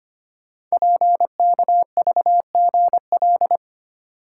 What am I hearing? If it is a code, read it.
Morse code, heard as a keyed tone, about 25 wpm, 690 Hz. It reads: PKVGL
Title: Code Groups - 5 Letters - 25wpm